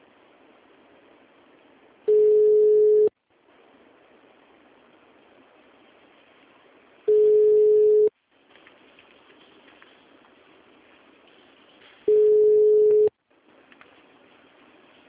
Ringback tone (Europe, 425 Hz).
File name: out- -2003- - - .4